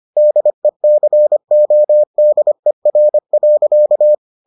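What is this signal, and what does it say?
Morse code, heard as a keyed tone, about 25 wpm, 600 Hz. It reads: DECODER.
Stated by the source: Morse edition of the Foundations of Amateur Radio episodes.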